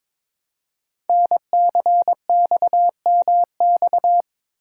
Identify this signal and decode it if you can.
Morse code, heard as a keyed tone, about 22 wpm, 700 Hz. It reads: NCXMX
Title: Code Groups - 5 Letters - 22wpm